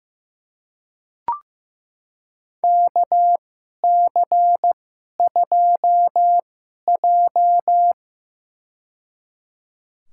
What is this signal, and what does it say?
Morse code, heard as a keyed tone, about 15 wpm, spaced as for 11 wpm, 700 Hz. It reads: KC2J
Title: USA Call Signs - 2x Character Spacing - 15wpm